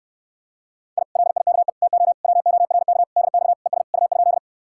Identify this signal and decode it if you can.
Morse code, heard as a keyed tone, about 69 wpm, 700 Hz. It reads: I LIKE TO WORK DX ES U?